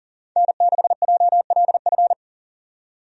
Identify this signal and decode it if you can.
Morse code, heard as a keyed tone, about 39 wpm, 700 Hz. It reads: N6JLF
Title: USA Call Signs - 5 Characters - 40wpm